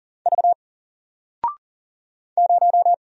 Morse code, heard as a keyed tone, about 40 wpm, 700 Hz. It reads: V 0
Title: Single Letter-Number 40wpm